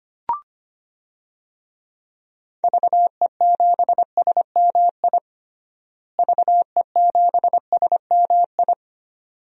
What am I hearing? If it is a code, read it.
Morse code, heard as a keyed tone, about 25 wpm, 700 Hz. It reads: VE7SMI VE7SMI
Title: DXCC-100 Call Signs - 6 Characters - Repeated 2x - 25wpm